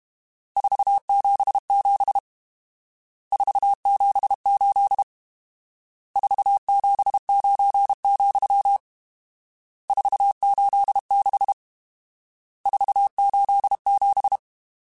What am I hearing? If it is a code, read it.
Morse code, heard as a keyed tone, about 32 wpm, 800 Hz. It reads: 477 478 479, 486 487
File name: RSTs / RST-mp3s / Various RST @30